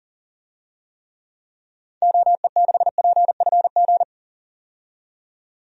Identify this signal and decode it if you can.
Morse code, heard as a keyed tone, about 40 wpm, 700 Hz. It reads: OE6PFG
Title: DXCC-100 Call Signs - 6 Characters - 40wpm